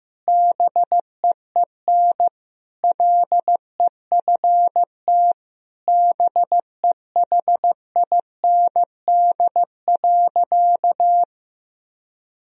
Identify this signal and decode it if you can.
Morse code, heard as a keyed tone, about 15 wpm, 700 Hz. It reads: BEEN LEFT BEHIND.
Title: Sentences from Top 1000 Words - Review Part 1 - 15wpm